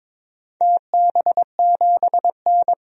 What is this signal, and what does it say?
Morse code, heard as a keyed tone, about 22 wpm, 700 Hz. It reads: TB7N
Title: DXCC-100 Call Signs - 4 Characters - 22wpm